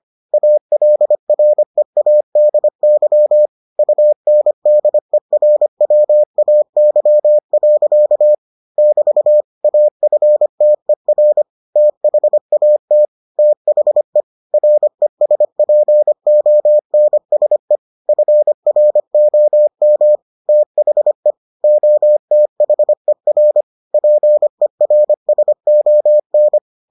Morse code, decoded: ALREADY UNDERWAY. <BT> AFTER THAT THE RESPONSE FROM THE OTHER PERSON

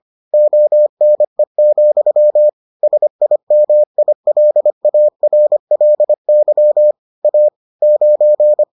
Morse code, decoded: ONE, SIMILARLY A 9

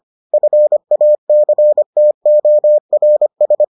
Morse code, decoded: FACTORS

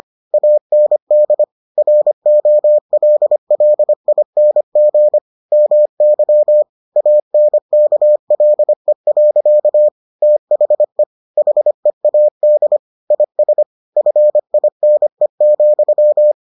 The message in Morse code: AND ROLLING MY ANKLE. THE HEAD IS FINE,

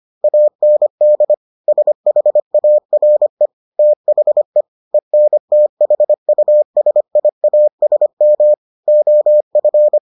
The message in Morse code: AND SHARE THE ENTHUSIASM OF

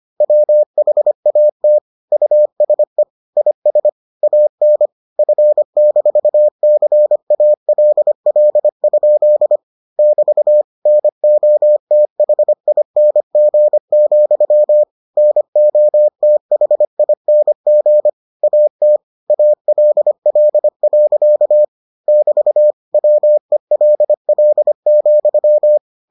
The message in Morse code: WHAT USE IS AN F-CALL? <BT> NOTHING, NOTHING AT ALL. <BT> WELL,

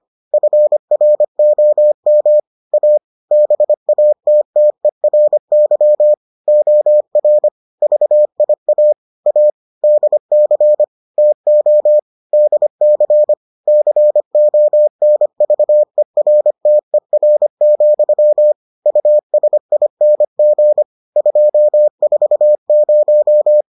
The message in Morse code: FROM A BATTERY OR VIA A DC TO DC CONVERTER, USING 240

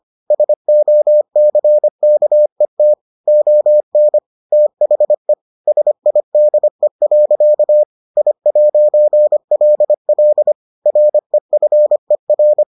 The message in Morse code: SOCKET ON THE SIDE. I'LL REFER

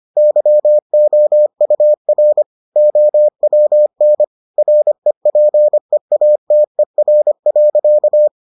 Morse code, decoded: YOUR OWN REPEATER.

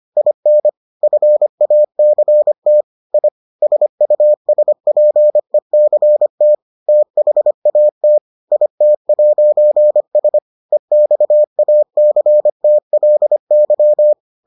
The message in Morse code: IN FACT I SUSPECT THAT IT'S EXACTLY

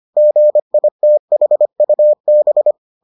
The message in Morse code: GITHUB

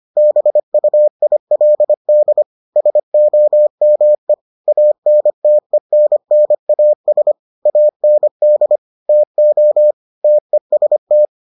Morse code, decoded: BUILD SOME ANTENNAS AND TO TEST

